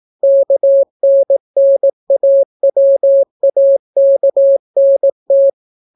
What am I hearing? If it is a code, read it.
Morse code, heard as a keyed tone, about 18 wpm, 550 Hz. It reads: KNNAWAKNT